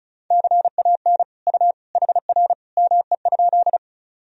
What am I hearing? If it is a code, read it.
Morse code, heard as a keyed tone, about 34 wpm, 700 Hz. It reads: CAN U HR ME?
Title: Sentences from Top 300 Words and Abbreviations 35wpm